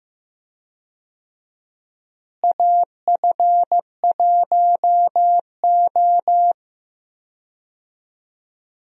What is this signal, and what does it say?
Morse code, heard as a keyed tone, about 15 wpm, 700 Hz. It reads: AF1O
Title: USA Call Signs - 4 Characters - Repeated 2x - 15wpm